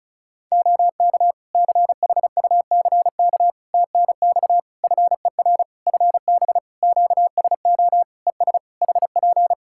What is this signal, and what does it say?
Morse code, heard as a keyed tone, about 36 wpm, 700 Hz. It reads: OK CHUCK TNX FER FB QSO ES HP
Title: Practice QSO - Exchange 4 - Ending - 35wpm